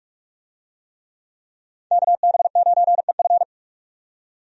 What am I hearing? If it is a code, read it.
Morse code, heard as a keyed tone, about 47 wpm, 700 Hz. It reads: KB9EF
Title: USA Call Signs - 5 Characters - 45wpm